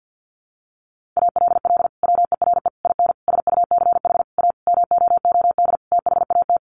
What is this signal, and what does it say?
Morse code, heard as a keyed tone, about 50 wpm, 700 Hz. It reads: ALL WERE IN SUCH A MOOD THAT
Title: Sentences from Top 1100 Words 50wpm